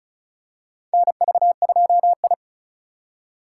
Morse code, heard as a keyed tone, about 34 wpm, 700 Hz. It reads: NV2I
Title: USA Call Signs - 4x Character Spacing - 35wpm